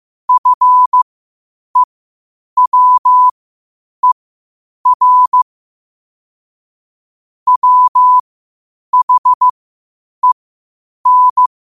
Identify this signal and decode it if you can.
Morse code, heard as a keyed tone, about 15 wpm, spaced as for 8 wpm, 1000 Hz. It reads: FEWER WHEN
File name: qotdfiles / 8wpm